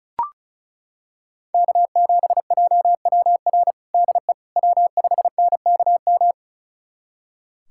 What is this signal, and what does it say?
Morse code, heard as a keyed tone, about 36 wpm, 700 Hz. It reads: K7JWR DE W5NKM
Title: USA Callsign DE Callsign 35wpm